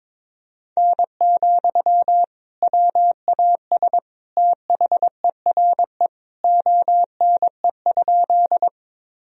Morse code, heard as a keyed tone, about 21 wpm, 700 Hz. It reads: N, WAS THERE ONE?
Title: Sentences from Top 100 Words and Abbreviations 22wpm